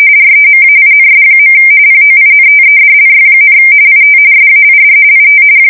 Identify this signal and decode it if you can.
RTTY, about 45.45 baud. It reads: REMAINS. POINTEDLY, IT SAID IT
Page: 170 Shift 2125/2295